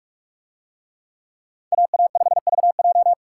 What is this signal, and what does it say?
Morse code, heard as a keyed tone, about 44 wpm, 700 Hz. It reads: AA5VJ